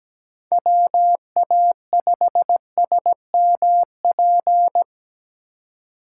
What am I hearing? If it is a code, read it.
Morse code, heard as a keyed tone, about 17 wpm, 700 Hz. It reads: WA5SMP
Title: USA Call Signs - 6 Characters - 2x Character Spacing - 17wpm